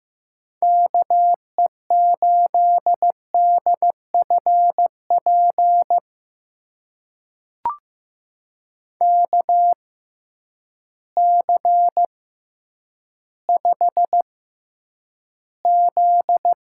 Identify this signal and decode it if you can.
Morse code, heard as a keyed tone, about 15 wpm, spaced as for 9 wpm, 700 Hz. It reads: KE8DFP KC5Z
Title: USA Call Signs - 6 Characters - 6x Character Spacing - 15wpm